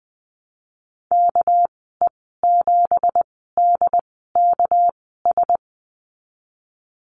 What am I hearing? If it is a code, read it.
Morse code, heard as a keyed tone, about 20 wpm, spaced as for 15 wpm, 700 Hz. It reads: KE7DKS
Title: 200 Lesson 32g - Letter k - Call Signs 20wpm